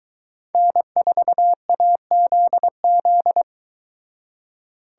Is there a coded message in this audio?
Yes: Morse code from 0.5 to 3.4 s, heard as a keyed tone, about 23 wpm, 700 Hz. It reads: N4AZZ